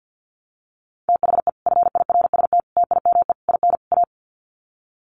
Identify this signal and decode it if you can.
Morse code, heard as a keyed tone, about 50 wpm, 700 Hz. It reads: THE FIRST TIME IN A
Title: Sets of 5 Words Encore 50wpm